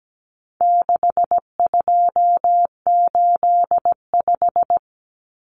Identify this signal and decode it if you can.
Morse code, heard as a keyed tone, about 17 wpm, 700 Hz. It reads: 6285
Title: Common Cut Numbers - 4-Digit 17wpm